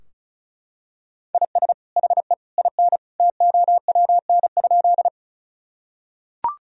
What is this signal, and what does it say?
Morse code, heard as a keyed tone, about 34 wpm, 700 Hz. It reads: IS HE IN TOWN?
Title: Sentences from Top 1000 Words - Review Part 2 - 35wpm